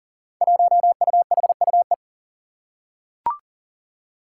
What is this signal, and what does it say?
Morse code, heard as a keyed tone, about 40 wpm, 700 Hz. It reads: 1UHUE
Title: Five Challenging Letter-Number Combinations 40wpm